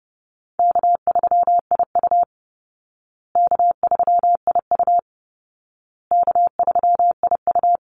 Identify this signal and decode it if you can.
Morse code, heard as a keyed tone, about 30 wpm, 700 Hz. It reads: K3IU K3IU K3IU
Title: Easy Call Signs Repeated 3x - 30wpm